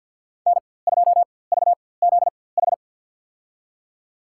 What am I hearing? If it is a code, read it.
Morse code, heard as a keyed tone, about 50 wpm, spaced as for 23 wpm, 700 Hz. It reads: N2VZH
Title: USA Call Signs - 4x Character Spacing - 50wpm